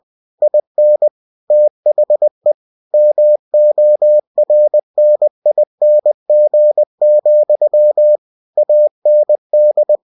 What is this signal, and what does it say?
Morse code, heard as a keyed tone, about 20 wpm, 600 Hz. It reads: IN THE MORNING, AND
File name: cw-books / winnie-the-pooh / winnie-20wpm-page-0015